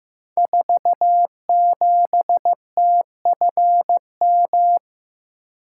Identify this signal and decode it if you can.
Morse code, heard as a keyed tone, about 15 wpm, 700 Hz. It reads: 47TFM
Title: Code Groups - 5 Letter:Numbers - 15wpm